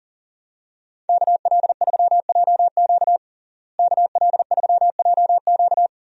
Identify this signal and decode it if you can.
Morse code, heard as a keyed tone, about 40 wpm, 700 Hz. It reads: KL3JQ KL3JQ
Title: USA Call Signs - 5 Characters - Repeated 2x - 40wpm